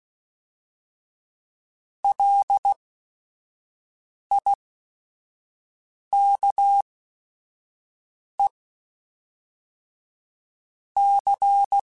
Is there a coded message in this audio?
Yes: Morse code from 2.0 to 11.8 s, heard as a keyed tone, about 16 wpm, spaced as for 5 wpm, 800 Hz. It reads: LIKE C